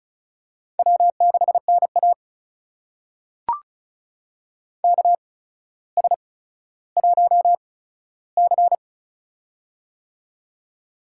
Morse code, heard as a keyed tone, about 36 wpm, spaced as for 15 wpm, 700 Hz. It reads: W6NA KS1C
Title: USA Call Signs - 4 Characters - 8x Character Spacing - 35wpm